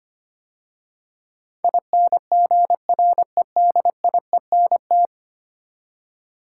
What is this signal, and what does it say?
Morse code, heard as a keyed tone, about 25 wpm, 700 Hz. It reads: INGREDIENT
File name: 10 Letter Words - First 3 Letters Given - 25wpm